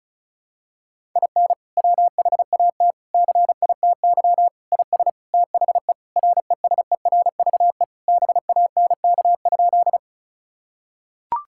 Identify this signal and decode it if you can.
Morse code, heard as a keyed tone, about 36 wpm, 700 Hz. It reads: IN WHAT CITY IS THE RESERVE BANK?
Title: Sentences from Top 1000 Words - Review Part 6 - 35wpm